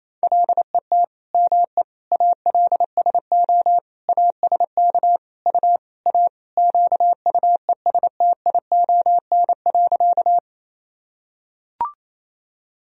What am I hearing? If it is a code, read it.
Morse code, heard as a keyed tone, about 28 wpm, 700 Hz. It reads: LET ME ALSO ASK U A QUESTION.
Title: Sentences from Top 400 Words and Abbreviations 28wpm